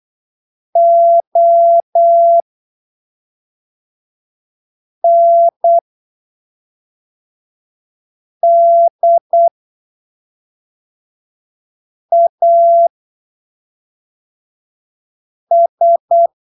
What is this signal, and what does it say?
Morse code, heard as a keyed tone, about 8 wpm, spaced as for 3 wpm, 680 Hz. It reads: ONDAS
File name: arquivos / texto / texto 10 08ppm